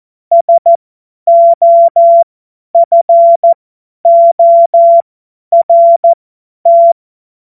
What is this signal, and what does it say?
Morse code, heard as a keyed tone, about 14 wpm, spaced as for 10 wpm, 680 Hz. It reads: SOFORT
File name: Morsekurs / Witze / Witz 09. August 2022 50 F70